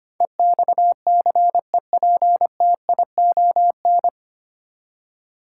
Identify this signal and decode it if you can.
Morse code, heard as a keyed tone, about 25 wpm, 700 Hz. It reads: EXCEPTION
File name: practice / 25wpm / 9 Letter Words - First 3 Letters Given - 25wpm